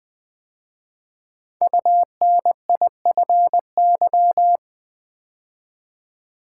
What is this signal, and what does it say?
Morse code, heard as a keyed tone, about 20 wpm, 700 Hz. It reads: UNIFY
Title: 5 Letter Words - First 3 Letters Given - 20wpm